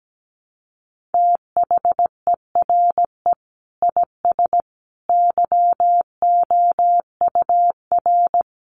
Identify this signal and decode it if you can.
Morse code, heard as a keyed tone, about 17 wpm, 700 Hz. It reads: THERE IS YOUR
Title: Sentences from Top 800 Words 17wpm